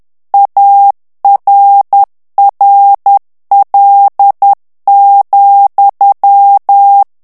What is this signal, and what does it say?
Morse code, heard as a keyed tone, about 10 wpm, 800 Hz. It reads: ARRL,
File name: ANGLAIS texte clair 3 vitesse 10 mots minute